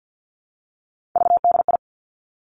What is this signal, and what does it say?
Morse code, heard as a keyed tone, about 50 wpm, 700 Hz. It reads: VDI
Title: Three Challenging Letter-Number Combinations 50wpm